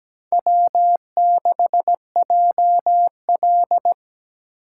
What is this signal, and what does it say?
Morse code, heard as a keyed tone, about 20 wpm, 700 Hz. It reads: W6JL
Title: Speed Racing - USA 1x2 Call Signs 17wpm